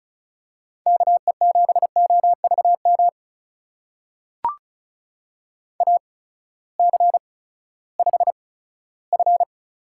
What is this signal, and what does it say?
Morse code, heard as a keyed tone, about 36 wpm, spaced as for 18 wpm, 700 Hz. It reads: KE7OVM AC5F
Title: USA Call Signs - 8x Character Spacing - 35wpm